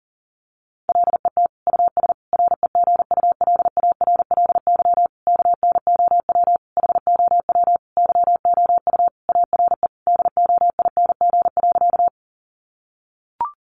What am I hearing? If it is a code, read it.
Morse code, heard as a keyed tone, about 40 wpm, 700 Hz. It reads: LET US REGULARLY KNOW HOW YOU ARE DOING.
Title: Sentences from Top 1400 Words 40wpm